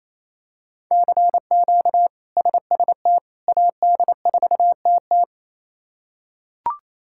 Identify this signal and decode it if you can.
Morse code, heard as a keyed tone, about 28 wpm, 700 Hz. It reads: CQ SST AD4TT
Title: K1USN SST QSO Practice - 28wpm